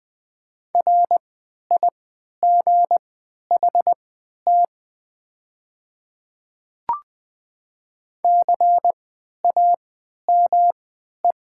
Morse code, heard as a keyed tone, about 20 wpm, spaced as for 11 wpm, 700 Hz. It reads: RIGHT CAME
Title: 3x Character Spacing - Top 200 Words - 20wpm